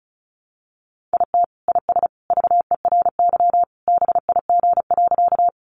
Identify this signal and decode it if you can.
Morse code, heard as a keyed tone, about 34 wpm, 700 Hz. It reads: IT IS VERY BIG.
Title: Easy Phrases 35wpm